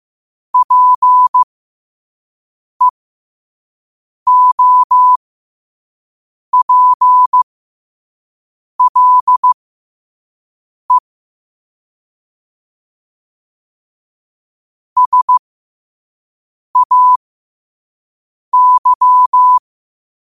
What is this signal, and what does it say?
Morse code, heard as a keyed tone, about 15 wpm, spaced as for 5 wpm, 1000 Hz. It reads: PEOPLE SAY
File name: qotdfiles / 5wpm